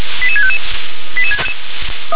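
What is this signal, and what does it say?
Five-tone selective call (ZVEI), heard as quick runs of five tones, tones of about 70 ms. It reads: 88655
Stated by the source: ZVEI five-tone selective call received on an SDR.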